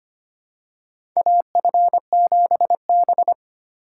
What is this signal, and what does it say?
Morse code, heard as a keyed tone, about 25 wpm, 700 Hz. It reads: AF7B
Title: USA Call Signs - 4 Characters - 3x Character Spacing - 25wpm